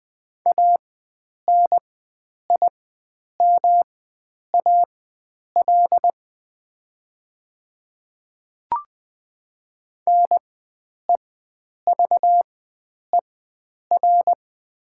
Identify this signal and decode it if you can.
Morse code, heard as a keyed tone, about 20 wpm, spaced as for 9 wpm, 700 Hz. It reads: ANIMAL NEVER
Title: 4x Character Spacing - Top 300 Words - 20wpm